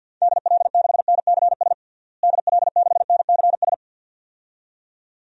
Morse code, heard as a keyed tone, about 51 wpm, 700 Hz. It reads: DL6NCS DL6NCS
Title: DXCC-100 Call Signs - 6 Characters - Repeated 2x - 50wpm